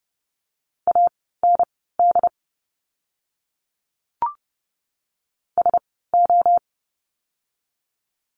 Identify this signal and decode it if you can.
Morse code, heard as a keyed tone, about 30 wpm, spaced as for 17 wpm, 700 Hz. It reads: AND SO
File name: course / 30:10wpm / 230 Lesson 37f - Letter x - Words 30wpm